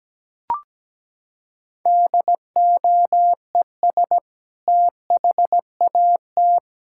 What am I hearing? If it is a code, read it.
Morse code, heard as a keyed tone, about 17 wpm, 700 Hz. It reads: DOES THAT
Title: Sentences from Top 1000 Words - Review Part 6 - 17wpm